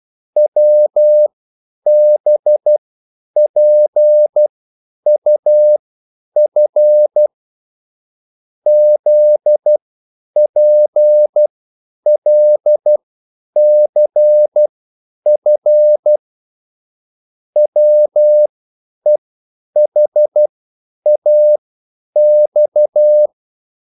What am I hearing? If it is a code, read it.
Morse code, heard as a keyed tone, about 12 wpm, spaced as for 9 wpm, 600 Hz. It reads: WBPUF ZPLCF WEHAX
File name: QSL / CW lesson